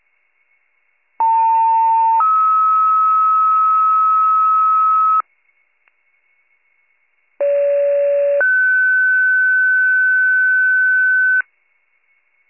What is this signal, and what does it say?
Two-tone page, heard as a one-second tone, then a three-second tone, about 900 then 1290 Hz, about 570 then 1500 Hz. It is heard from 1.2 to 11.4 s.